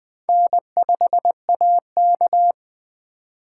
Morse code, heard as a keyed tone, about 21 wpm, 700 Hz. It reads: N5AK